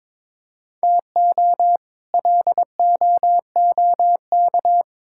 Morse code, heard as a keyed tone, about 22 wpm, 700 Hz. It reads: TO LOOK
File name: practice / 22wpm / Sets of 6 Words - 22wpm